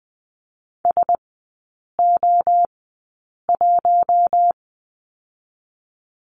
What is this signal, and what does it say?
Morse code, heard as a keyed tone, about 20 wpm, spaced as for 8 wpm, 700 Hz. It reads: SO1
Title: 041 Lesson 8e - Number 1 - Three Characters 20wpm